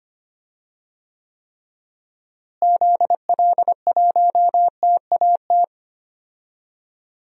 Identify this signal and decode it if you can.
Morse code, heard as a keyed tone, about 25 wpm, 700 Hz. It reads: ZL1TAT
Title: DXCC-100 Call Signs - 6 Characters - Repeated 3x - 25wpm